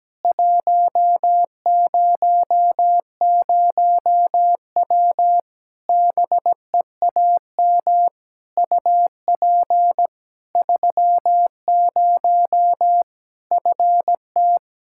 Morse code, heard as a keyed tone, about 17 wpm, 700 Hz. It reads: 100W BEAM UP 30 FT
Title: Practice QSO - Exchange 2 - Info - 17wpm